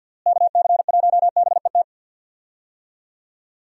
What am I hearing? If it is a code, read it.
Morse code, heard as a keyed tone, about 48 wpm, 700 Hz. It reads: KC1BET
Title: USA Call Signs - 6 Characters - Repeated 2x - 50wpm